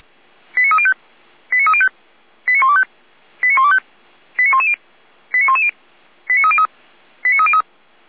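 Five-tone selective call (ZVEI), heard as quick runs of five tones, tones of about 70 ms. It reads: 89305 89305 89125 89125 89110 89110 89303 89303
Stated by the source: ZVEI five-tone call 89097 received by SDR.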